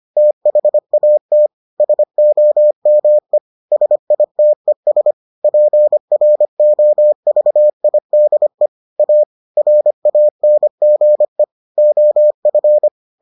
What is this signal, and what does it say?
Morse code, heard as a keyed tone, about 25 wpm, 600 Hz. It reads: THAT SOME SITES PROVIDE A RANGE OF